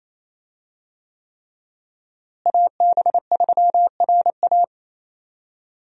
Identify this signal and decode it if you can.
Morse code, heard as a keyed tone, about 28 wpm, 700 Hz. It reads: AB3RA
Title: USA Call Signs - 5 Characters - Repeated 3x - 28wpm